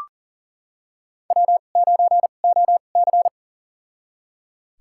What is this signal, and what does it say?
Morse code, heard as a keyed tone, about 37 wpm, spaced as for 28 wpm, 700 Hz. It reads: W9OC
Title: USA Call Signs - 2x Character Spacing - 40wpm